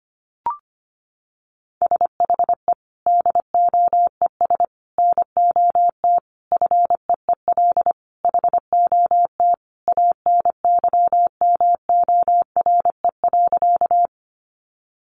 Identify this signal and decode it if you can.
Morse code, heard as a keyed tone, about 25 wpm, 700 Hz. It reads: SHE DOES NOT FEEL HOT ANYMORE.
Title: Additional Easy Phrases 25wpm